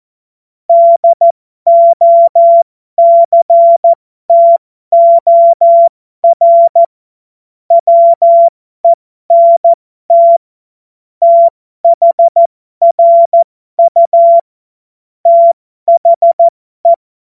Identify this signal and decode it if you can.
Morse code, heard as a keyed tone, about 14 wpm, spaced as for 12 wpm, 680 Hz. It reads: DOCTOR WENT THRU THE